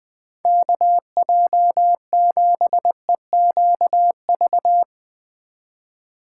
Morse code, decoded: KJ7EQV

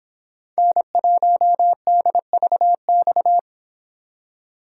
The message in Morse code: N1DVX

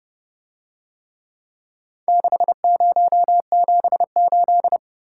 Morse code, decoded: 6078

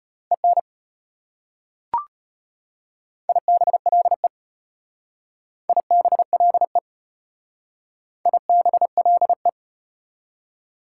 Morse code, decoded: EN IBLE IBLE IBLE

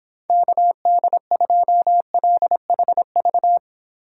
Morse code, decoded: KD2LHV

